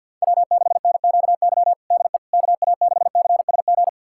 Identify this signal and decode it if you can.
Morse code, heard as a keyed tone, about 52 wpm, 700 Hz. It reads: W6NQY DE KA6CSG
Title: USA Callsign DE Callsign 50wpm